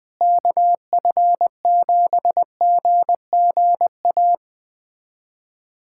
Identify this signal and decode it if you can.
Morse code, heard as a keyed tone, about 20 wpm, 700 Hz. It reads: KF7GGA
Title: USA Callsign DE Callsign 20wpm